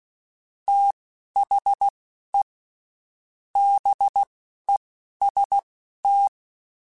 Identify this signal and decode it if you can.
Morse code, heard as a keyed tone, about 16 wpm, spaced as for 12 wpm, 800 Hz. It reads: THE BEST
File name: Quotes Spaced / Quotes Spaced-mp3s / Temper sp @15